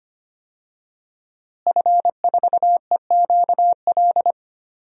Morse code, heard as a keyed tone, about 25 wpm, 700 Hz. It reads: F4EQL
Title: DXCC-100 Call Signs - 5 Characters - Repeated 3x - 25wpm